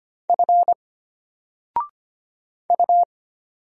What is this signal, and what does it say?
Morse code, heard as a keyed tone, about 25 wpm, 700 Hz. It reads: F U
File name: Single-Letter-Number-25wpm